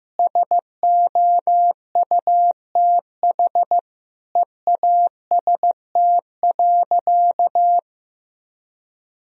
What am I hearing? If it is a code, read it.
Morse code, heard as a keyed tone, about 16 wpm, 700 Hz. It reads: SOUTH EAST.
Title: Sentences from Top 1000 Words - Review Part 3 - 15wpm